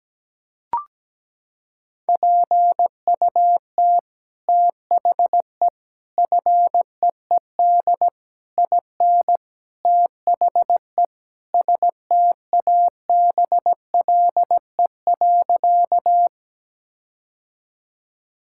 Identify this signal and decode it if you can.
Morse code, heard as a keyed tone, about 17 wpm, 700 Hz. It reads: PUT THE FEED IN THE STABLE.
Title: Sentences from Top 1600 Words 17wpm